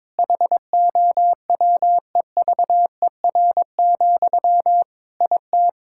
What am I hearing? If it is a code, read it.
Morse code, heard as a keyed tone, about 22 wpm, 700 Hz. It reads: HOWEVER, IT
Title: Sentences from Top 1600 Words 22wpm